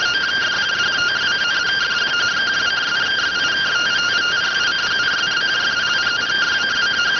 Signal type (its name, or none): two-tone teleprinter signal